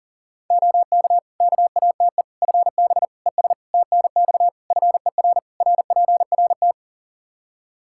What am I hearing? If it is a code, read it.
Morse code, heard as a keyed tone, about 40 wpm, 700 Hz. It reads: OK KATE FB ES TNX FER RPRT
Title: Practice QSO - Exchange 2 - Info - 40wpm